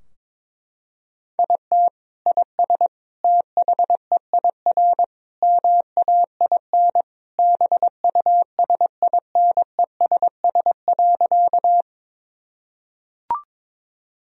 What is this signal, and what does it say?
Morse code, heard as a keyed tone, about 21 wpm, 700 Hz. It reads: IT IS THEIR MAIN BUSINESS.
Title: Sentences from Top 1000 Words - Review Part 2 - 22wpm